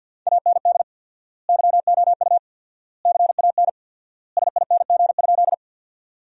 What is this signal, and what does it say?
Morse code, heard as a keyed tone, about 50 wpm, 700 Hz. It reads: AND YOU CAN SING?
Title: 4x Word-Spacing - Sentences from Top 400 Words - 50wpm